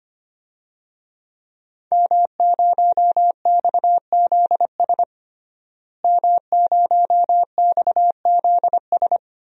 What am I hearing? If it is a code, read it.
Morse code, heard as a keyed tone, about 25 wpm, 700 Hz. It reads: M0XZS M0XZS
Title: DXCC-100 Call Signs - 5 Characters - Repeated 2x - 25wpm